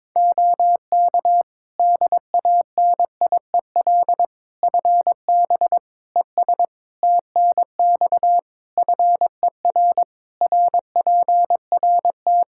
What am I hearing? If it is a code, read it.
Morse code, heard as a keyed tone, about 22 wpm, 700 Hz. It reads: OK DANIEL FB ES TNX FER RPRT
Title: Practice QSO - Exchange 2 - Info - 22wpm